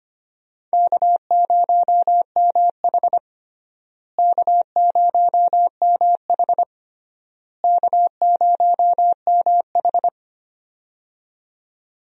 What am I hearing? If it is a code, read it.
Morse code, heard as a keyed tone, about 25 wpm, 700 Hz. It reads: K0MH K0MH K0MH
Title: USA Call Signs - 4 Characters - Repeated 3x - 25wpm